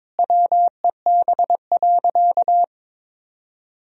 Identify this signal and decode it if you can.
Morse code, heard as a keyed tone, about 22 wpm, 700 Hz. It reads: WEB.
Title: Sentences from Top 1000 Words - Review Part 4 - 22wpm